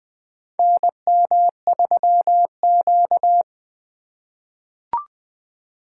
Morse code, heard as a keyed tone, about 20 wpm, 700 Hz. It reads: NM3Q
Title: USA Call Signs - 4 Characters - 6x Character Spacing - 20wpm